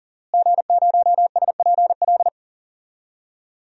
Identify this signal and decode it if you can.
Morse code, heard as a keyed tone, about 40 wpm, 700 Hz. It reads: G0SPL
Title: DXCC-100 Call Signs - 5 Characters - Repeated 3x - 40wpm